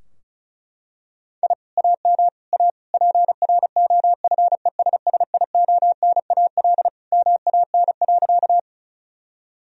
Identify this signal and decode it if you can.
Morse code, heard as a keyed tone, about 34 wpm, 700 Hz. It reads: I AM A PROFESSIONAL MAN.
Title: Sentences from Top 1000 Words - Review Part 4 - 35wpm